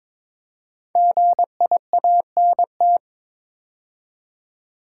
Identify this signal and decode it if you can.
Morse code, heard as a keyed tone, about 22 wpm, 690 Hz. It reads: GIANT
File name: 5 Letter Words - First 3 Letters Given - 22wpm